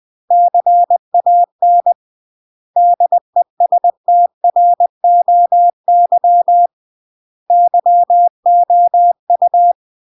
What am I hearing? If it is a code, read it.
Morse code, heard as a keyed tone, about 20 wpm, 700 Hz. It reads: CAN DESTROY YOU
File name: CW / jodorowsky / 82 commandments 20-20